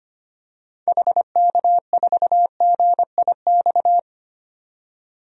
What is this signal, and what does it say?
Morse code, heard as a keyed tone, about 25 wpm, 700 Hz. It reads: HK4GIX
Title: DXCC-100 Call Signs - 6 Characters - 25wpm